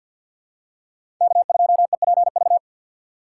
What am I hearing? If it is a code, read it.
Morse code, heard as a keyed tone, about 51 wpm, 700 Hz. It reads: K2EPV
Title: USA Call Signs - 5 Characters - 2x Character Spacing - 50wpm